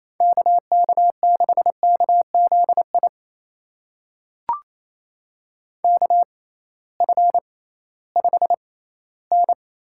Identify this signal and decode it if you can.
Morse code, heard as a keyed tone, about 28 wpm, spaced as for 16 wpm, 700 Hz. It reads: KK6KZI KF5N
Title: USA Call Signs - 6 Characters - 6x Character Spacing - 28wpm